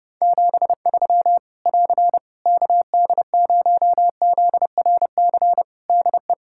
Morse code, heard as a keyed tone, about 30 wpm, 700 Hz. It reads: 73 <AR> KD0ZRC DE
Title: K1USN SST QSO Practice 30wpm